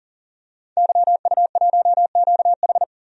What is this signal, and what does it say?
Morse code, heard as a keyed tone, about 40 wpm, 700 Hz. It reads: YU1QH
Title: DXCC-100 Call Signs - 5 Characters - 40wpm